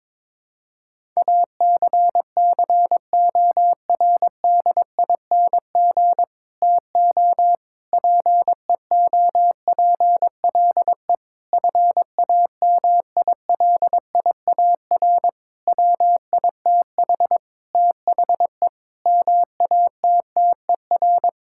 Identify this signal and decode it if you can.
Morse code, heard as a keyed tone, about 22 wpm, 700 Hz. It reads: ACCORDING TO PEOPLE FAMILIAR WITH THE MATTER
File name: Sentences with 7-Word Phrases - 22wpm